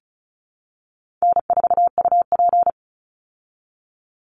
Morse code, heard as a keyed tone, about 34 wpm, 700 Hz. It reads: N4UP
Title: Easy Call Signs Repeated 3x - 35wpm